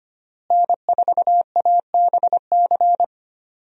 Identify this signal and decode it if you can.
Morse code, heard as a keyed tone, about 25 wpm, 700 Hz. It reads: N4ABC
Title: USA Call Signs - 5 Characters - Repeated 3x - 25wpm